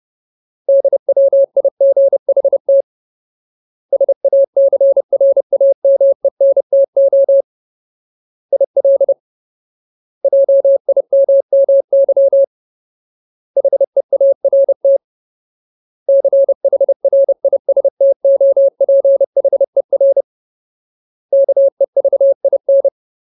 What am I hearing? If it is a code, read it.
Morse code, heard as a keyed tone, about 30 wpm, 550 Hz. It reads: DWIGHT SACRAMENTO IL JIMMY HEART CHRISTOPHER KEVIN
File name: Ultimate Word List / 30WPM Ultimate